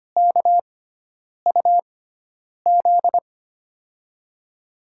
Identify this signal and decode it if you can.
Morse code, heard as a keyed tone, about 25 wpm, spaced as for 9 wpm, 700 Hz. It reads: KUZ
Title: USA Call Signs - 6 Characters - 6x Character Spacing - 25wpm